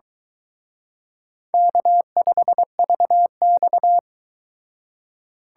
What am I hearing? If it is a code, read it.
Morse code, heard as a keyed tone, about 23 wpm, 700 Hz. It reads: K5VX